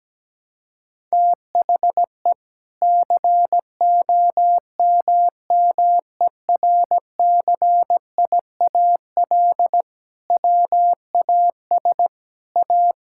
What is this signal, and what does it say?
Morse code, heard as a keyed tone, about 17 wpm, 700 Hz. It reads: THE COMMERCIAL WAS A
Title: Sentences from Top 1000 Words - Review Part 5 - 17wpm